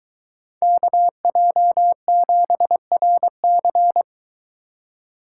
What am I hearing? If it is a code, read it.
Morse code, heard as a keyed tone, about 23 wpm, 700 Hz. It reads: KJ7RC